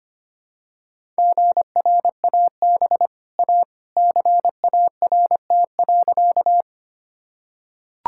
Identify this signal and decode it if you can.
Morse code, heard as a keyed tone, about 25 wpm, 700 Hz. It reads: GRAB A CART.
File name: Easy Sentences with Hints - 25wpm